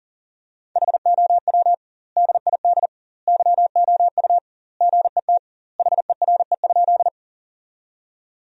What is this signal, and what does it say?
Morse code, heard as a keyed tone, about 40 wpm, 700 Hz. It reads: HOW DID YOU GET HERE?
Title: 2x Word-Spacing - Sentences from Top 200 Words - 40wpm